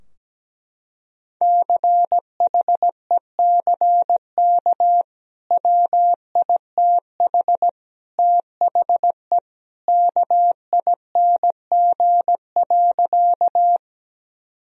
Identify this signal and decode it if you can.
Morse code, heard as a keyed tone, about 17 wpm, 700 Hz. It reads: CHECK WITH THE KING.
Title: Sentences from Top 1000 Words - Review Part 3 - 17wpm